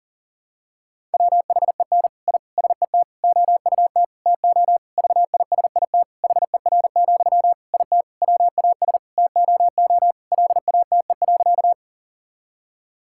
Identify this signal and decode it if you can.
Morse code, heard as a keyed tone, about 40 wpm, 700 Hz. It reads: WHEN I SET OUT TO VISIT HER, IT WAS TOO LATE.
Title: Sentences from Top 1000 Words - Review Part 6 - 40wpm